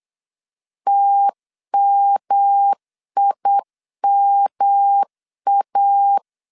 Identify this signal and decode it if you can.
Morse code, heard as a keyed tone, about 8 wpm, 800 Hz. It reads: TMIMA
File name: CW / DET / 1 3 1500C S B 8wpm 4141